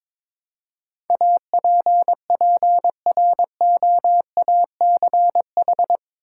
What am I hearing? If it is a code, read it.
Morse code, heard as a keyed tone, about 22 wpm, 700 Hz. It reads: APPROACH
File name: practice / 22wpm / Eight Letter Words - 22wpm